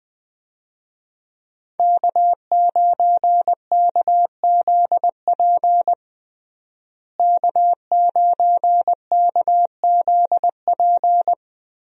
Morse code, heard as a keyed tone, about 21 wpm, 700 Hz. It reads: K9KZP K9KZP
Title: USA Call Signs - 5 Characters - Repeated 3x - 20wpm